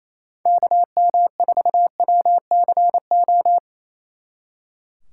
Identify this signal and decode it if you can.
Morse code, heard as a keyed tone, about 34 wpm, 700 Hz. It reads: KM4WCO
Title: Speed Racing - USA 2x3 Call Signs 28wpm